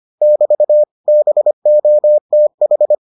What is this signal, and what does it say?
Morse code, heard as a keyed tone, about 25 wpm, 600 Hz. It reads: <BT> BOTH